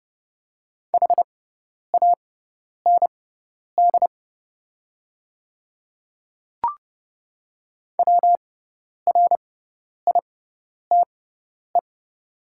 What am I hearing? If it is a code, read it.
Morse code, heard as a keyed tone, about 30 wpm, spaced as for 10 wpm, 700 Hz. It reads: HAND WRITE
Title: 6x Character Spacing - Top 300 Words - 30wpm